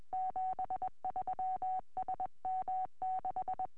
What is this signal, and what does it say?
Morse code, heard as a keyed tone, about 21 wpm, 750 Hz. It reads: 73 SM6